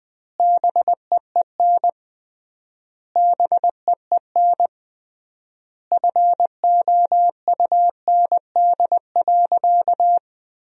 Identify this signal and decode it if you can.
Morse code, heard as a keyed tone, about 20 wpm, 700 Hz. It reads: BEEN BEEN FOUND.
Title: 3x Word-Spacing - Sentences from Top 300 Words - 20wpm